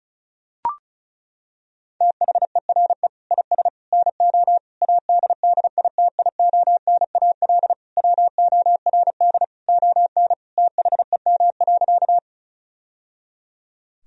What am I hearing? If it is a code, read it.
Morse code, heard as a keyed tone, about 35 wpm, 700 Hz. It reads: THERE IS NO ADDITIONAL WORD ON THEM.
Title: Sentences from Top 1000 Words - Review Part 4 - 35wpm